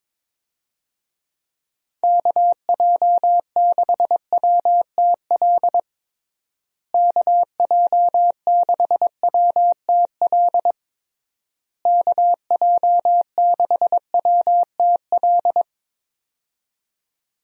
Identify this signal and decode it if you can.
Morse code, heard as a keyed tone, about 22 wpm, 700 Hz. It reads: KJ6WTL KJ6WTL KJ6WTL
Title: USA Call Signs - 6 Characters - Repeated 3x - 22wpm